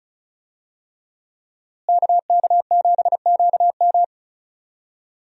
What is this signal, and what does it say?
Morse code, heard as a keyed tone, about 36 wpm, 700 Hz. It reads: KK7QM
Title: USA Call Signs - 5 Characters - Repeated 3x - 35wpm